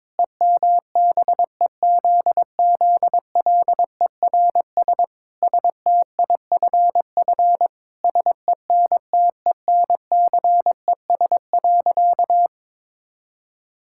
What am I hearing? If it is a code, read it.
Morse code, heard as a keyed tone, about 22 wpm, 700 Hz. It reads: EMBEZZLERS STIFF SENTENCES.
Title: Pangram Challenge v2 - 22wpm